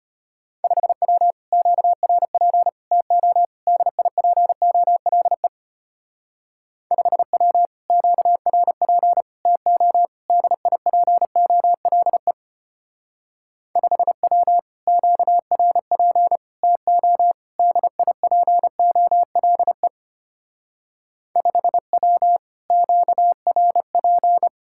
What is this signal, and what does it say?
Morse code, heard as a keyed tone, about 31 wpm, 700 Hz. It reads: 5W QRP TO DIPOLE 5W QRP TO DIPOLE 5W QRP TO DIPOLE 5W QRP
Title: Speed-Racing Common QSO Phrases Encore - 25wpm